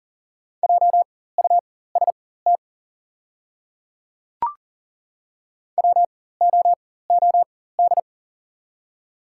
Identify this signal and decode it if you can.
Morse code, heard as a keyed tone, about 40 wpm, spaced as for 19 wpm, 700 Hz. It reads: JUST WOOD
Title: 4x Character Spacing - Top 300 Words - 40wpm